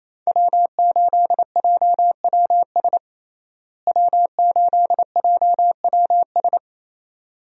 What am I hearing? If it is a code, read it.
Morse code, heard as a keyed tone, about 28 wpm, 700 Hz. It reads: W8JWS W8JWS
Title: USA Call Signs - 5 Characters - Repeated 2x - 28wpm